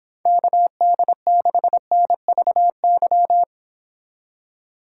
Morse code, heard as a keyed tone, about 26 wpm, 700 Hz. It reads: KD6NVY